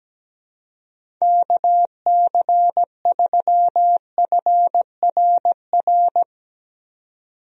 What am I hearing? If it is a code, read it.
Morse code, heard as a keyed tone, about 17 wpm, 700 Hz. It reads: KC3FRR
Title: USA Call Signs - 6 Characters - 17wpm